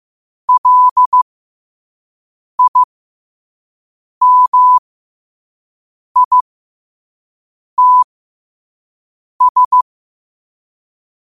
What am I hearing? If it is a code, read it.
Morse code, heard as a keyed tone, about 15 wpm, spaced as for 5 wpm, 1000 Hz. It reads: LIMITS